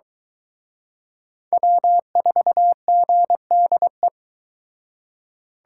Morse code, heard as a keyed tone, about 23 wpm, 700 Hz. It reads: W4GDE